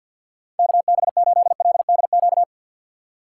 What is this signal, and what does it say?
Morse code, heard as a keyed tone, about 60 wpm, 700 Hz. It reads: KB8LDQ
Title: Speed Racing - USA Call Signs 50wpm